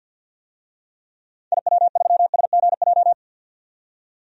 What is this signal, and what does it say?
Morse code, heard as a keyed tone, about 53 wpm, 700 Hz. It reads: IW3SGJ
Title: DXCC-100 Call Signs - 6 Characters - 50wpm